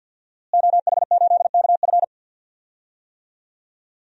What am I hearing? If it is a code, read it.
Morse code, heard as a keyed tone, about 50 wpm, 700 Hz. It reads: OH8KF
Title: DXCC-100 Call Signs - 5 Characters - Repeated 3x - 50wpm